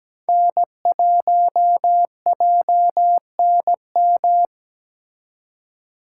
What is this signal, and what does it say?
Morse code, heard as a keyed tone, about 20 wpm, 700 Hz. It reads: N1JNM
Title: Speed Racing - USA Call Signs 17wpm